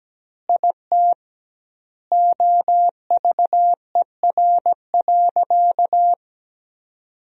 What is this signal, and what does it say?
Morse code, heard as a keyed tone, about 17 wpm, 700 Hz. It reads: IT OVER.
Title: 2x Word-Spacing - Sentences from Top 500 Words - 17wpm